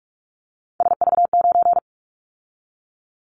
Morse code, heard as a keyed tone, about 46 wpm, 700 Hz. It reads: SV9
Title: Callsign Prefixes for Top 100 DXCC - 45wpm